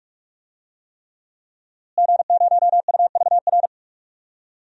Morse code, heard as a keyed tone, about 43 wpm, 700 Hz. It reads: G0UVR
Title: DXCC-100 Call Signs - 5 Characters - Repeated 3x - 45wpm